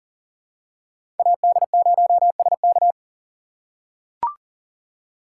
Morse code, heard as a keyed tone, about 40 wpm, 700 Hz. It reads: AD0SK
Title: USA Call Signs - 2x Character Spacing - 40wpm